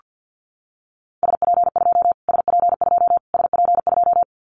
Morse code, heard as a keyed tone, about 50 wpm, 700 Hz. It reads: SP2 SP2 SP2
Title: Callsign Prefixes for Top 25 DXCC - Repeated 3x - 50wpm